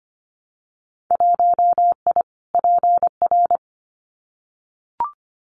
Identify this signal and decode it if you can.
Morse code, heard as a keyed tone, about 25 wpm, 700 Hz. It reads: 1I PR
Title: Winter Field Day Exchanges Encore 25wpm